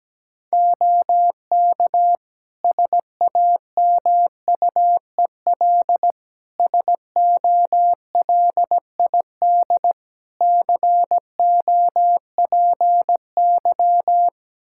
Morse code, heard as a keyed tone, about 17 wpm, 700 Hz. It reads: OK SAMUEL SOLID COPY
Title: K1USN SST QSO Practice 17wpm